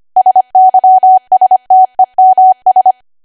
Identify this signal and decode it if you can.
Morse code, heard as a keyed tone, about 25 wpm, 750 Hz. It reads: SYSTEMS
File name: Archive / 25 WPM / 25WPM